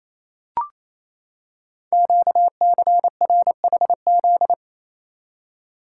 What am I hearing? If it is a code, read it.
Morse code, heard as a keyed tone, about 28 wpm, 700 Hz. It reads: QCRHZ
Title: Code Groups - 5 Letters - 28wpm